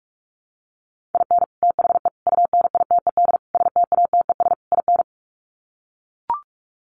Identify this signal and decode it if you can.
Morse code, heard as a keyed tone, about 44 wpm, 700 Hz. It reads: IN THE UNITED STATES IN
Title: Sets of 5 Words 45wpm